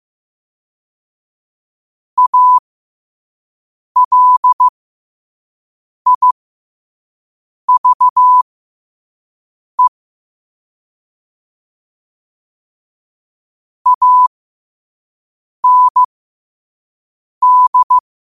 Morse code, heard as a keyed tone, about 15 wpm, spaced as for 5 wpm, 1000 Hz. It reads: ALIVE AND